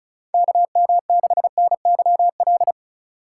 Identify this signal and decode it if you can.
Morse code, heard as a keyed tone, about 36 wpm, 700 Hz. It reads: KM6NYL
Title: USA Call Signs - 6 Characters - 3x Character Spacing - 35wpm